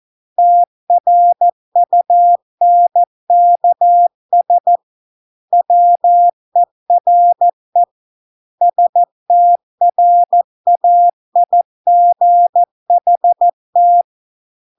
Morse code, heard as a keyed tone, about 14 wpm, 700 Hz. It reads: TRUNKS WERE STRAIGHT